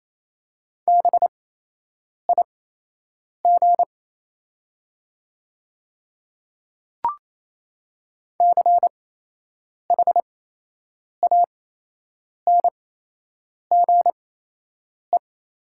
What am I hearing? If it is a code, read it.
Morse code, heard as a keyed tone, about 28 wpm, spaced as for 8 wpm, 700 Hz. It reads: BIG CHANGE
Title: 8x Character Spacing - Top 200 Words - 28wpm